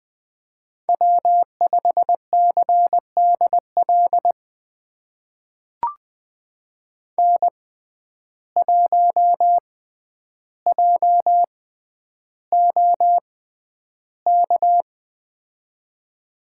Morse code, heard as a keyed tone, about 20 wpm, spaced as for 10 wpm, 700 Hz. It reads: W5CDL N1JOK
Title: USA Call Signs - 6x Character Spacing - 20wpm